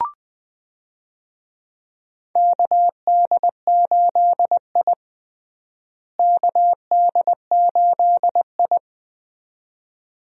Morse code, heard as a keyed tone, about 20 wpm, 700 Hz. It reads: KD8I KD8I
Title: USA Call Signs - 4 Characters - Repeated 2x - 20wpm